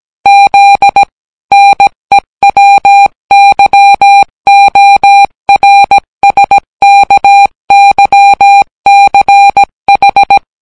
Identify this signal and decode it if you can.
Morse code, heard as a keyed tone, about 17 wpm, 800 Hz. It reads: Z NEWYORSKYCH